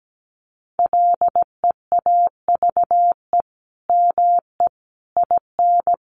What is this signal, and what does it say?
Morse code, heard as a keyed tone, about 17 wpm, 700 Hz. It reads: LEAVE ME IN
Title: Sentences from Top 1500 Words 17wpm